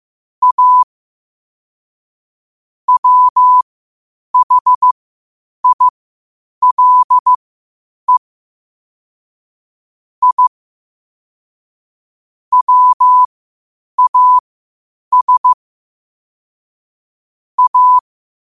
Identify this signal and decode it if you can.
Morse code, heard as a keyed tone, about 15 wpm, spaced as for 8 wpm, 1000 Hz. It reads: A WHILE I WAS A
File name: qotdfiles / 8wpm